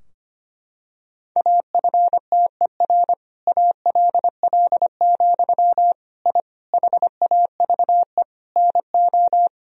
Morse code, heard as a keyed tone, about 25 wpm, 700 Hz. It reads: AFTER ALL, I HAVE NO
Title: Sentences from Top 1000 Words - Review Part 3 - 25wpm